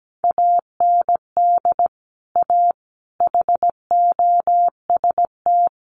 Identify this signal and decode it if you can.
Morse code, heard as a keyed tone, about 17 wpm, 700 Hz. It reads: AND A HOST
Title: Sets of 5 Words Encore2 17wpm